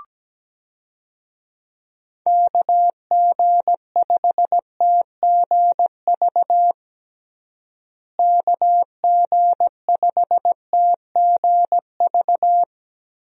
Morse code, heard as a keyed tone, about 17 wpm, 700 Hz. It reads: KG5TGV KG5TGV
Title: USA Call Signs - 6 Characters - Repeated 3x - 17wpm